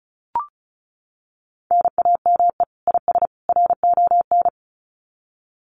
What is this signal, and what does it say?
Morse code, heard as a keyed tone, about 34 wpm, 700 Hz. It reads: NAME IS RON
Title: Common QSO Phrases 35wpm